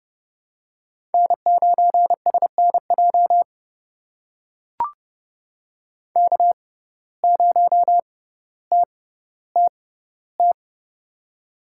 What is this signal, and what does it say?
Morse code, heard as a keyed tone, about 30 wpm, spaced as for 15 wpm, 700 Hz. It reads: N9SNJ K0TTT
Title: USA Call Signs - 5 Characters - 6x Character Spacing - 30wpm